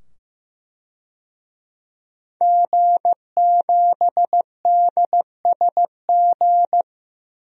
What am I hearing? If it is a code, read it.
Morse code, heard as a keyed tone, about 15 wpm, 700 Hz. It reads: G7DSG
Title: DXCC-100 Call Signs - 5 Characters - 15wpm